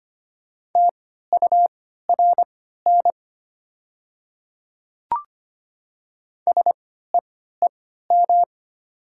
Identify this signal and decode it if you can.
Morse code, heard as a keyed tone, about 25 wpm, spaced as for 14 wpm, 690 Hz. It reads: TURN SEEM
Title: 3x Character Spacing - Top 300 Words - 25wpm